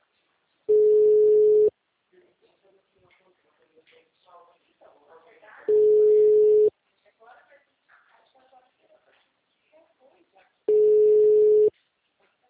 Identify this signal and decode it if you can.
Ringback tone (Europe, 425 Hz), with speech.